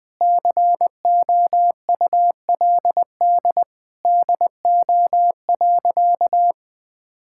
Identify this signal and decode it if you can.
Morse code, heard as a keyed tone, about 20 wpm, 700 Hz. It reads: COULD DO.
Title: Sentences from Top 1000 Words - Review Part 6 - 20wpm